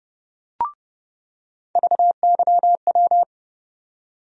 Morse code, heard as a keyed tone, about 30 wpm, 700 Hz. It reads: VYW
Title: USA Call Sign Postfixes - 3 Letters 30wpm